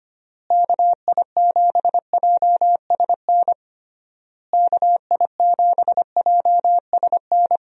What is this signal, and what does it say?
Morse code, heard as a keyed tone, about 25 wpm, 700 Hz. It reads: KI7JSN KI7JSN
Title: USA Call Signs - 6 Characters - Repeated 3x - 25wpm